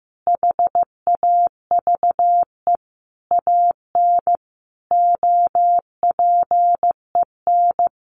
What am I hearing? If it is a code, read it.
Morse code, heard as a keyed tone, about 15 wpm, 700 Hz. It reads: HAVE AN OPEN
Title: Sentences from Top 1400 Words 15wpm